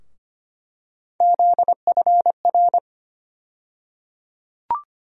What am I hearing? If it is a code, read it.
Morse code, heard as a keyed tone, about 25 wpm, 700 Hz. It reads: ZFR
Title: USA Call Sign Postfixes - 3 Letters 25wpm